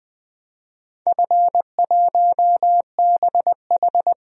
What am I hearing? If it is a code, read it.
Morse code, heard as a keyed tone, about 20 wpm, 700 Hz. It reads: F1BH
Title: DXCC-100 Call Signs - 4 Characters - 20wpm